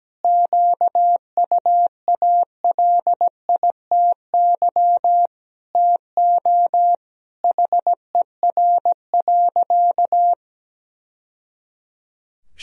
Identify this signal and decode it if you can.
Morse code, heard as a keyed tone, about 17 wpm, 700 Hz. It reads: QUALITY TO HER.
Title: Sentences from Top 1000 Words - Review Part 4 - 17wpm